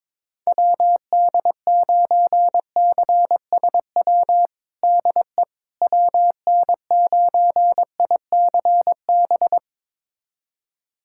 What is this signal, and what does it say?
Morse code, heard as a keyed tone, about 22 wpm, 700 Hz. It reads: WD9CSW DE WN9ICB
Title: USA Callsign DE Callsign 22wpm